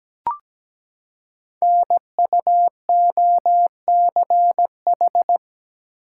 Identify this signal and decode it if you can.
Morse code, heard as a keyed tone, about 17 wpm, 700 Hz. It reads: NUOCH
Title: Code Groups - 5 Letters - 17wpm